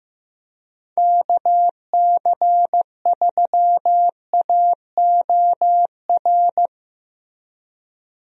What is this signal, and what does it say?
Morse code, heard as a keyed tone, about 15 wpm, 700 Hz. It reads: KC3AOR
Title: USA Call Signs - 6 Characters - Repeated 2x - 15wpm